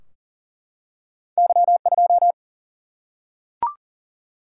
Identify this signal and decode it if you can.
Morse code, heard as a keyed tone, about 38 wpm, 700 Hz. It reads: Y2
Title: Single-Letter Single-Number 40wpm